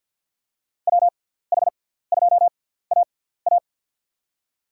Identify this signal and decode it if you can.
Morse code, heard as a keyed tone, about 52 wpm, spaced as for 17 wpm, 700 Hz. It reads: WH2AA
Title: USA Call Signs - 6x Character Spacing - 50wpm